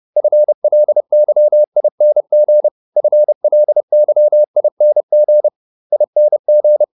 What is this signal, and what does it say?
Morse code, heard as a keyed tone, about 30 wpm, 600 Hz. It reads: FLYING FLYING ING